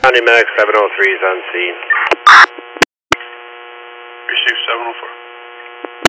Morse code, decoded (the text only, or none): none